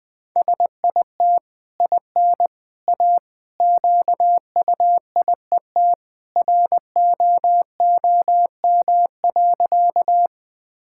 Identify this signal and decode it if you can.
Morse code, heard as a keyed tone, about 20 wpm, 700 Hz. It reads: SIT IN A QUIET ROOM.
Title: Easy Sentences with Hints - 20wpm